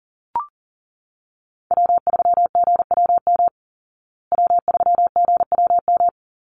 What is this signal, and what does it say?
Morse code, heard as a keyed tone, about 40 wpm, 700 Hz. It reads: W3GWM W3GWM
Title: Call Signs Repeated 2x - 40wpm